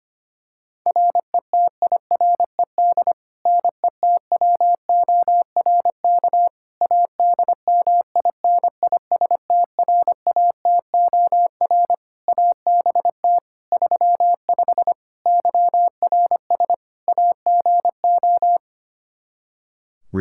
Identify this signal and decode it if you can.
Morse code, heard as a keyed tone, about 25 wpm, 700 Hz. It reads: RETIRED NETWORK ADMINISTRATOR ABT 35 YRS AGO
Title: Practice QSO - Exchange 3 - More Info - 25wpm